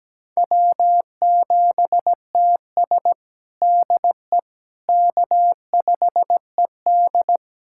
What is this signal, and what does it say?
Morse code, heard as a keyed tone, about 17 wpm, 700 Hz. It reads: W7TS DE K5ED
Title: USA Callsign DE Callsign - 4 Characters - 17wpm